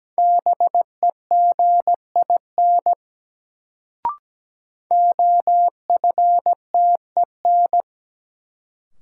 Morse code, heard as a keyed tone, about 17 wpm, 700 Hz. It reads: BEGIN OFTEN
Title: Sentences from Top 300 Words - New Word Drill - 17wpm